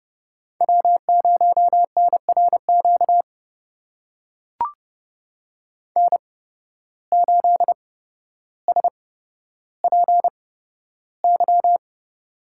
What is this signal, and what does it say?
Morse code, heard as a keyed tone, about 30 wpm, spaced as for 13 wpm, 700 Hz. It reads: W0NRQ N8SPY
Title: USA Call Signs - 5 Characters - 8x Character Spacing - 30wpm